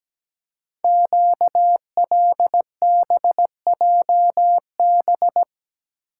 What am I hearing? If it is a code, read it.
Morse code, heard as a keyed tone, about 17 wpm, 700 Hz. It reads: QLBJB
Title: Code Groups - 5 Letters - 17wpm